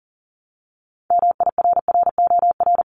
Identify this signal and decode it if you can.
Morse code, heard as a keyed tone, about 42 wpm, 700 Hz. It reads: MIRROR
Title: Multi-letter Phonemes Repeated 3x 40wpm